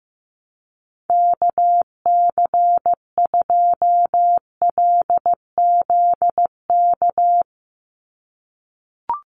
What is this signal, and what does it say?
Morse code, heard as a keyed tone, about 15 wpm, 700 Hz. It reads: KC2LZK
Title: USA Call Signs 15wpm